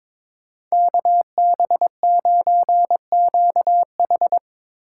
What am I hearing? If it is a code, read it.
Morse code, heard as a keyed tone, about 25 wpm, 700 Hz. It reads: KB9QH
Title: Speed Racing - USA 2x2 Call Signs 22wpm